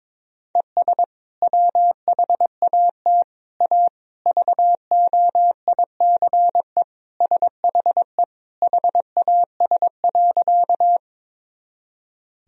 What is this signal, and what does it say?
Morse code, heard as a keyed tone, about 21 wpm, 700 Hz. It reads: ES WHAT A VOICE SHE HAS.
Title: Sentences from Top 500 Words and Abbreviations 22wpm